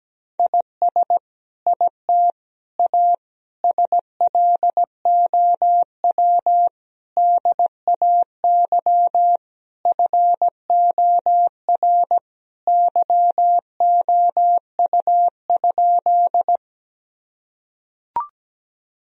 Morse code, heard as a keyed tone, about 17 wpm, 700 Hz. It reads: IS IT A SLOW DAY FOR YOU?
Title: Sentences from Top 1000 Words - Review Part 2 - 17wpm